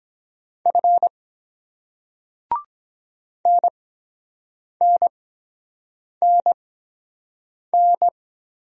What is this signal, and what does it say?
Morse code, heard as a keyed tone, about 21 wpm, spaced as for 7 wpm, 700 Hz. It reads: F NNNN
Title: Speed-Racing Single Letters 17wpm